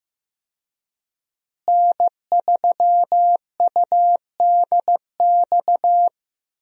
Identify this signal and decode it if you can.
Morse code, heard as a keyed tone, about 15 wpm, 700 Hz. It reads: N3UDX
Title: USA Call Signs - 5 Characters - Repeated 3x - 15wpm